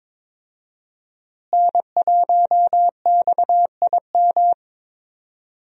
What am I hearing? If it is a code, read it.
Morse code, heard as a keyed tone, about 22 wpm, 700 Hz. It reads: N1XIM
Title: USA Call Signs - 5 Characters - Repeated 2x - 22wpm